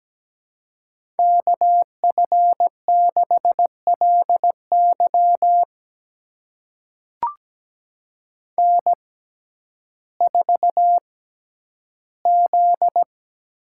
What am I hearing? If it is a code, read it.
Morse code, heard as a keyed tone, about 17 wpm, spaced as for 10 wpm, 700 Hz. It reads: KF6LY N4Z
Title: USA Call Signs - 5 Characters - 6x Character Spacing - 17wpm